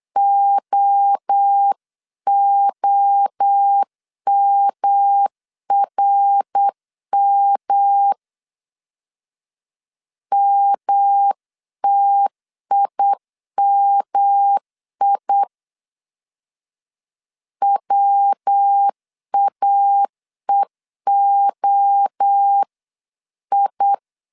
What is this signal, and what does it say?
Morse code, heard as a keyed tone, about 8 wpm, 800 Hz. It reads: OOMRM MTIMI WAEOI